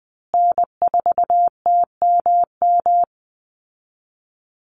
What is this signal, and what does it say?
Morse code, heard as a keyed tone, about 20 wpm, 700 Hz. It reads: N4TMM
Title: Call Signs Repeated 3x - 20wpm